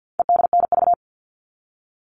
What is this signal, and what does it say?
Morse code, heard as a keyed tone, about 50 wpm, 700 Hz. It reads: EDNV
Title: All Cut Numbers - 4-Digit 50wpm